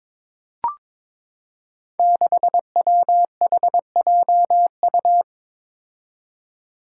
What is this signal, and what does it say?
Morse code, heard as a keyed tone, about 23 wpm, 700 Hz. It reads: 6WHJU
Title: Five Challenging Letter-Number Combinations 22wpm